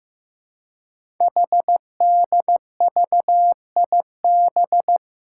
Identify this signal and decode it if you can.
Morse code, heard as a keyed tone, about 15 wpm, 700 Hz. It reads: HDVIB
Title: Code Groups - 5 Letters - 15wpm